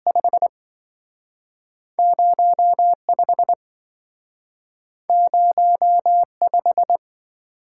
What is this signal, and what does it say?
Morse code, heard as a keyed tone, about 27 wpm, spaced as for 5 wpm, 700 Hz. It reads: 50 50 5